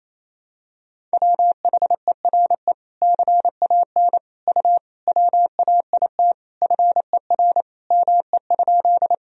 Morse code, heard as a keyed tone, about 28 wpm, 700 Hz. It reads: WHERE CAN U WAIT FER ME?
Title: Sentences from Top 500 Words and Abbreviations 28wpm